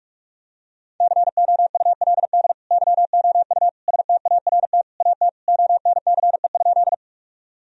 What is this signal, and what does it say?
Morse code, heard as a keyed tone, about 41 wpm, 700 Hz. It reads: COULD YOU START AT ONCE?
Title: Sentences from Top 1000 Words - Review Part 1 - 45wpm